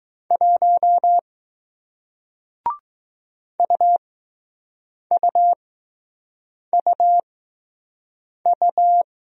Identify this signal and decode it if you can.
Morse code, heard as a keyed tone, about 20 wpm, spaced as for 6 wpm, 700 Hz. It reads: 1 UUUU